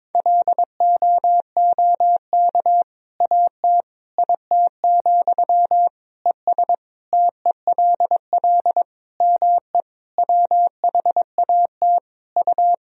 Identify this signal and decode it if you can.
Morse code, heard as a keyed tone, about 22 wpm, 700 Hz. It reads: LOOK AT IT, ES TELL ME WHAT U
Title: Sentences from Top 200 Words and Abbreviations 22wpm